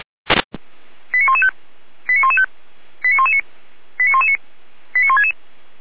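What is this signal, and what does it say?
Five-tone selective call (ZVEI), heard as quick runs of five tones, tones of about 70 ms. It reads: 89115 89115 89119 89119 89166
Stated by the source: ZVEI five-tone selective call received on an SDR.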